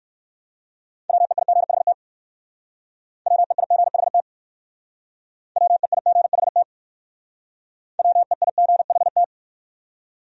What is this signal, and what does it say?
Morse code, heard as a keyed tone, about 57 wpm, 700 Hz. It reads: WEIGHT WEIGHT WEIGHT WEIGHT